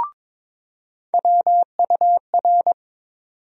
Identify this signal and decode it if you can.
Morse code, heard as a keyed tone, about 22 wpm, 700 Hz. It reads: WUR
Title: USA Call Sign Postfixes - 3 Letters 22wpm